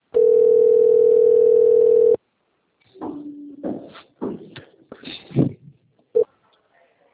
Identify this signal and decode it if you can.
Ringback tone (North America), 440 + 480 Hz.